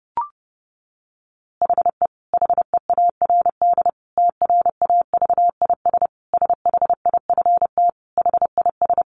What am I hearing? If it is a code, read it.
Morse code, heard as a keyed tone, about 30 wpm, 700 Hz. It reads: HE HEARD TRAVIS SHIFT HIS
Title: A Sound Of Thunder 30wpm